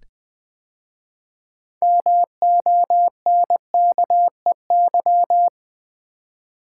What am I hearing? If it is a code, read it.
Morse code, heard as a keyed tone, about 20 wpm, 700 Hz. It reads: MONKEY
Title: 6 Letter Words - First 3 Letters Given - 20wpm